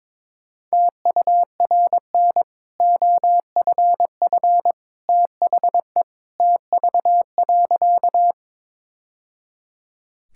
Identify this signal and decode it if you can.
Morse code, heard as a keyed tone, about 22 wpm, 690 Hz. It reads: TURN OFF THE TV.
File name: practice / 22wpm / Easy Sentences with Hints - 22wpm